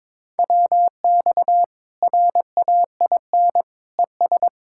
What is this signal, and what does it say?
Morse code, heard as a keyed tone, about 23 wpm, 700 Hz. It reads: WX RAIN ES
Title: Practice QSO - Exchange 2 - Info - 22wpm